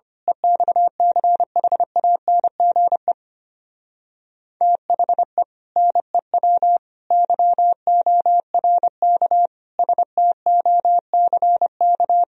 Morse code, heard as a keyed tone, about 25 wpm, 700 Hz. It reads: EXCHANGE THE NEW YORK STOCK
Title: Speed-Racing - Sets of 5 Words 25wpm